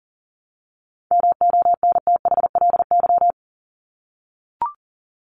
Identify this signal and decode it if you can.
Morse code, heard as a keyed tone, about 40 wpm, 700 Hz. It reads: MONTHLY
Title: LY Suffix - Top 100 Words with Base Words 40wpm